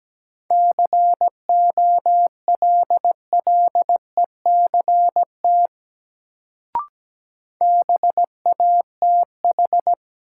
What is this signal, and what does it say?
Morse code, heard as a keyed tone, about 17 wpm, 700 Hz. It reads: COLLECT BATH
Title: Sentences from Top 1600 Words - New Word Drill - 17wpm